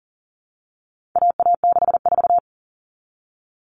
Morse code, heard as a keyed tone, about 40 wpm, 700 Hz. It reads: AA64
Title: All Cut Numbers - 4-Digit 40wpm